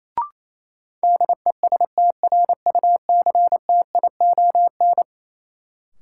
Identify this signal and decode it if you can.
Morse code, heard as a keyed tone, about 28 wpm, 700 Hz. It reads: DESTRUCTION
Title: Sets of 5 Words Encore - New Word Drill - 28wpm